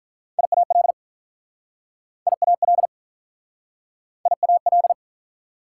Morse code, heard as a keyed tone, about 48 wpm, 700 Hz. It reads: IAL IAL IAL